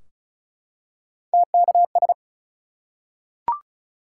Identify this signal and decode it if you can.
Morse code, heard as a keyed tone, about 36 wpm, 700 Hz. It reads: TKS
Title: Top 180 QSO Elements 35wpm